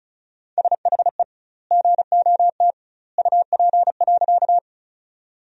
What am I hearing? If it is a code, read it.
Morse code, heard as a keyed tone, about 36 wpm, 700 Hz. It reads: SHE GOT UP.
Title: 2x Word-Spacing - Sentences from Top 300 Words - 35wpm